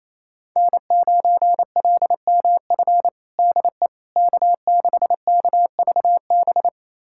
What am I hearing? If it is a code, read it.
Morse code, heard as a keyed tone, about 28 wpm, 700 Hz. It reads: N9LMF DE K6KVB
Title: USA Callsign DE Callsign - 5 Characters - 28wpm